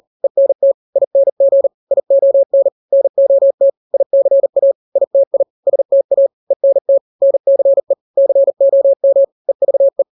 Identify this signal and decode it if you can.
Morse code, heard as a keyed tone, about 40 wpm, 550 Hz. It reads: ENT ING ION NOT ICA ITI STA ENT NCE COM EVE
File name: n-grams / 3-grams 40 35 WPM